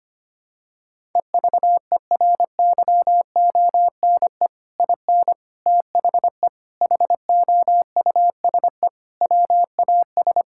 Morse code, heard as a keyed tone, about 25 wpm, 700 Hz. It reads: EVERYONE IN THE HOUSE WAS
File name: practice / 25wpm / Sentences with 3-Word Phrases - 25wpm